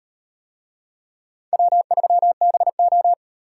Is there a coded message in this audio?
Yes: Morse code from 1.5 to 3.1 s, heard as a keyed tone, about 38 wpm, 700 Hz. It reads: W3BO